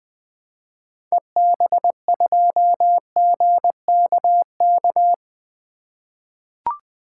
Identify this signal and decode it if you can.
Morse code, heard as a keyed tone, about 20 wpm, 700 Hz. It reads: EB2GKK
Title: 2023 SOTA Activator Call Signs - 20wpm